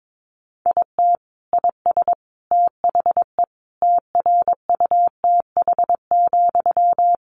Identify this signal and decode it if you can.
Morse code, heard as a keyed tone, about 22 wpm, 700 Hz. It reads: IT IS THE TRUTH,
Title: Sentences from Top 800 Words 22wpm